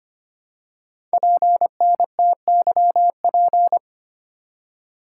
Morse code, heard as a keyed tone, about 25 wpm, 700 Hz. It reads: PNTYP
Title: Code Groups - 5 Letter:Numbers - 25wpm